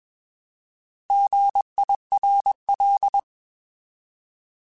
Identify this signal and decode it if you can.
Morse code, heard as a keyed tone, about 21 wpm, 800 Hz. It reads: GIRL
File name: Alphabetical / Alphabetical mp3s / Gs @20